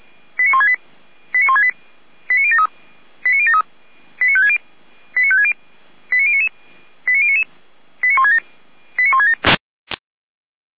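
Five-tone selective call (ZVEI), heard as quick runs of five tones, tones of about 70 ms. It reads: 89168 89168 89073 89073 89560 89560 89099 89099 89167 89167
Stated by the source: ZVEI five-tone call 89169 received by SDR.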